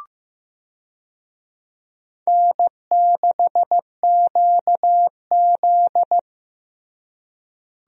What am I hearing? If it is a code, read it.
Morse code, heard as a keyed tone, about 15 wpm, 700 Hz. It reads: N6QZ
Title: USA Call Signs - 4 Characters - Repeated 2x - 15wpm